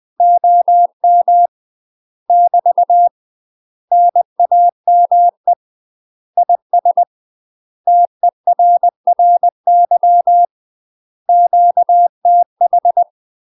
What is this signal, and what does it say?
Morse code, heard as a keyed tone, about 20 wpm, 700 Hz. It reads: OM <BT> NAME IS TERRY QTH
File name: UK MASTER QSOs TRAINING / 20WPM / UK MASTER QSOs 0052